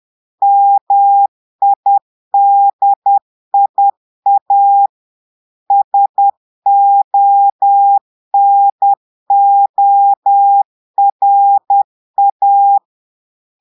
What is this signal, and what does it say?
Morse code, heard as a keyed tone, about 10 wpm, 800 Hz. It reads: MIDIA SONORA
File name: Portugais - texte en clair 1 - vitesse - 10 mots minute